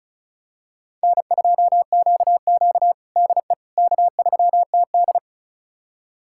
Morse code, heard as a keyed tone, about 36 wpm, 700 Hz. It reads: N2QQ DE K3TD
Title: USA Callsign DE Callsign - 4 Characters - 35wpm